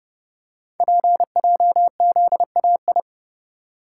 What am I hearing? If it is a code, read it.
Morse code, heard as a keyed tone, about 30 wpm, 700 Hz. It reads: PJZAI
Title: Code Groups - 5 Letters - 30wpm